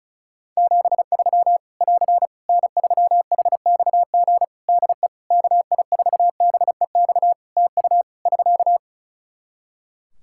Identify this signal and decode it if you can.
Morse code, heard as a keyed tone, about 34 wpm, 700 Hz. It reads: 73 <AR> N3HXG DE KI4BEX TU <SK>
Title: K1USN SST QSO Practice 35wpm